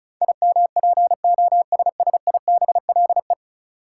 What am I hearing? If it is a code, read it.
Morse code, heard as a keyed tone, about 36 wpm, 700 Hz. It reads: IMPOSSIBLE
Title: Sentences from Top 1100 Words - New Word Drill - 35wpm